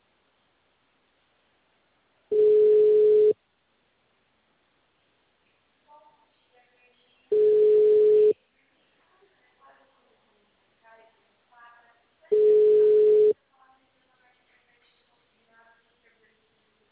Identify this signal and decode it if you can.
Ringback tone (Europe, 425 Hz), with speech.